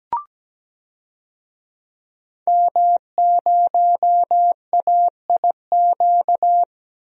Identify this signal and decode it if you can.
Morse code, heard as a keyed tone, about 17 wpm, 700 Hz. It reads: M0AIQ
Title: DXCC-100 Call Signs - 5 Characters - 17wpm